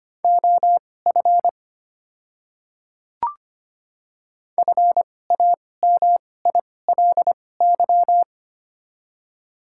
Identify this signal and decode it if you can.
Morse code, heard as a keyed tone, about 25 wpm, spaced as for 18 wpm, 700 Hz. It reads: OF FAMILY
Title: 2x Character Spacing - Top 500 Words - 25wpm